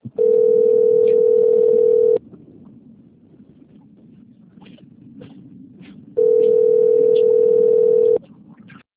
Ringback tone (North America), 440 + 480 Hz.
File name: rg-1600-unknown- - - .1716